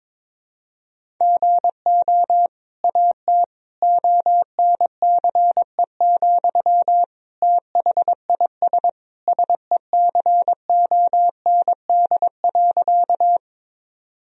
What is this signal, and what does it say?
Morse code, heard as a keyed tone, about 23 wpm, 700 Hz. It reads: GO AT ONCE, THIS SECOND.
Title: Sentences from Top 1000 Words - Review Part 1 - 22wpm